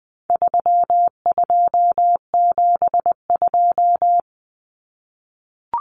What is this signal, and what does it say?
Morse code, heard as a keyed tone, about 20 wpm, 700 Hz. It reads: 3272
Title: Common Cut Numbers - 4-Digit 20wpm